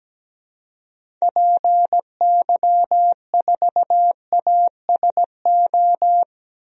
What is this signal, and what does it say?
Morse code, heard as a keyed tone, about 17 wpm, 700 Hz. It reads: PY4ASO
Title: DXCC-100 Call Signs - 6 Characters - 17wpm